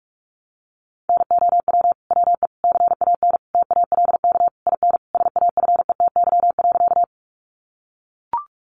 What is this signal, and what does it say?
Morse code, heard as a keyed tone, about 44 wpm, 700 Hz. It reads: NOW WE CAN TALK IN SAFETY.
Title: Sentences from Top 700 Words 45wpm